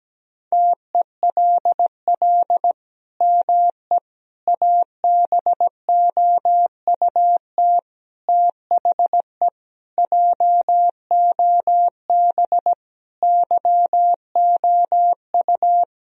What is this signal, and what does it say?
Morse code, heard as a keyed tone, about 17 wpm, 700 Hz. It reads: TELL ME ABOUT THE JOB YOU
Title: Sentences from Top 1000 Words - Review Part 3 - 17wpm